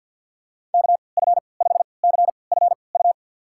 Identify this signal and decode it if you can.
Morse code, heard as a keyed tone, about 50 wpm, spaced as for 28 wpm, 700 Hz. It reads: KF5CFU
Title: USA Call Signs - 6 Characters - 3x Character Spacing - 50wpm